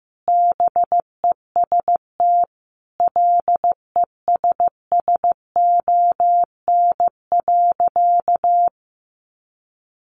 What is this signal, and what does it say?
Morse code, heard as a keyed tone, about 15 wpm, 700 Hz. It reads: BEST LESSON.
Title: Sentences from Top 1400 Words 15wpm